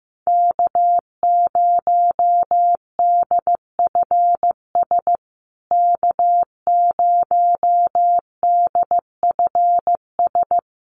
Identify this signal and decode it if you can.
Morse code, heard as a keyed tone, about 15 wpm, 700 Hz. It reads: K0DFS K0DFS
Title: USA Call Signs Repeated 3x 15wpm